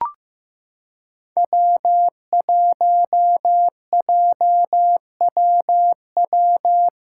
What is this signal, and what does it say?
Morse code, heard as a keyed tone, about 15 wpm, 700 Hz. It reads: W1JWW
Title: USA Callsign DE Callsign 15wpm